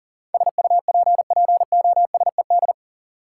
Morse code, heard as a keyed tone, about 40 wpm, 700 Hz. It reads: SUPPOSED
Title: Sets of 3 Words - New Word Drill - 40wpm